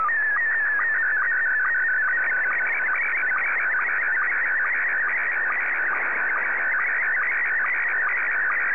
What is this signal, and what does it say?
SSTV (slow-scan television), Scottie 1.
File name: sstv / S1-VK7NMH